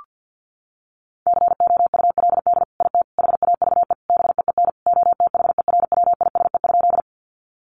Morse code, heard as a keyed tone, about 50 wpm, 700 Hz. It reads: COULD IT HAVE BEEN OTHERWISE?
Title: Sentences from Top 1200 Words 50wpm